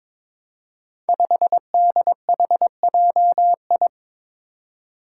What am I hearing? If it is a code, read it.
Morse code, heard as a keyed tone, about 22 wpm, 700 Hz. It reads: 5DHJI
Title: Five Challenging Letter-Number Combinations 22wpm